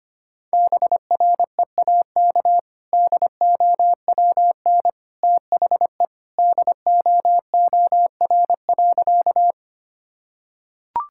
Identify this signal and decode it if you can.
Morse code, heard as a keyed tone, about 25 wpm, 700 Hz. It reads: BREAK DOWN THE DOOR.
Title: Sentences from Top 1000 Words - Review Part 6 - 25wpm